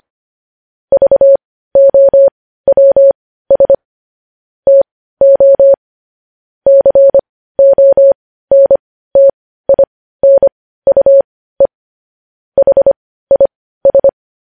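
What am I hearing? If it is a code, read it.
Morse code, heard as a keyed tone, about 25 wpm, spaced as for 15 wpm, 550 Hz. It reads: VOWS TO CONTINUE HIS
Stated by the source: Morse code news headlines sent at 15 words per minute.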